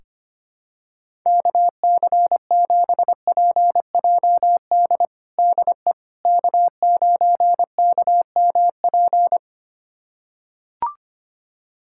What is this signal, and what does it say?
Morse code, heard as a keyed tone, about 25 wpm, 700 Hz. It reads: KC7PJD DE K9KMP
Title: USA Callsign DE Callsign 25wpm